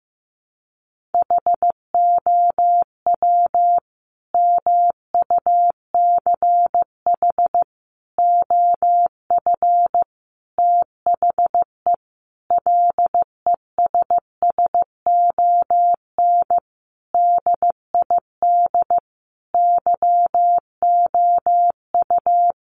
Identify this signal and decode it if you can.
Morse code, heard as a keyed tone, about 15 wpm, 700 Hz. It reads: HOW MUCH OF THE LESSON DID YOU
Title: Sentences from Top 1200 Words 15wpm